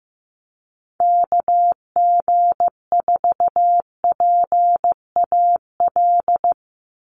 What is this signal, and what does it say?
Morse code, heard as a keyed tone, about 15 wpm, 700 Hz. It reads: KG4PAL
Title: USA Call Signs 15wpm